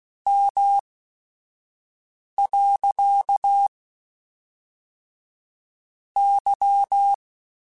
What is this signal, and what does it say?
Morse code, heard as a keyed tone, about 16 wpm, spaced as for 4 wpm, 800 Hz. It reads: M.Y
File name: QSOs / QSOs-mp3s / WA7QOX 1@5